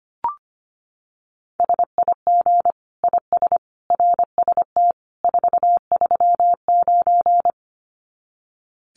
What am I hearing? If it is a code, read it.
Morse code, heard as a keyed tone, about 25 wpm, 700 Hz. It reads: SIG IS RST 439
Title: Common QSO Phrases 25wpm